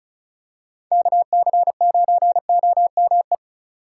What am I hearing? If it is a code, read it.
Morse code, heard as a keyed tone, about 34 wpm, 700 Hz. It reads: KC9OME
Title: USA Call Signs - 2x Character Spacing - 35wpm